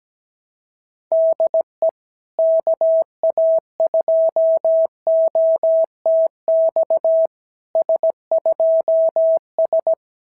Morse code, heard as a keyed tone, about 17 wpm, 650 Hz. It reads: DE KA2OTX S2S
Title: SOTA QSO Practice 17wpm